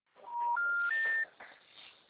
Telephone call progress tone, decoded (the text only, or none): special information tone